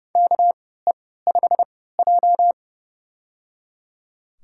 Morse code, heard as a keyed tone, about 29 wpm, spaced as for 17 wpm, 700 Hz. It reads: KE5J
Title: USA Call Signs - 4 Characters - 3x Character Spacing - 30wpm